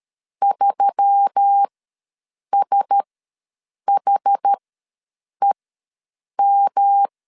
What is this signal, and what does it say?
Morse code, heard as a keyed tone, about 13 wpm, spaced as for 7 wpm, 800 Hz. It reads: 3SHEM
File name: LEZIONI / 5 LBVC / 5 LB ws 12wpm 3034